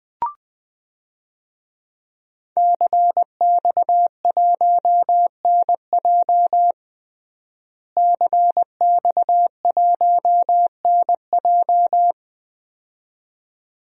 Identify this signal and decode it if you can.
Morse code, heard as a keyed tone, about 20 wpm, 700 Hz. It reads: CX1NJ CX1NJ
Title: DXCC-100 Call Signs - 5 Characters - Repeated 2x - 20wpm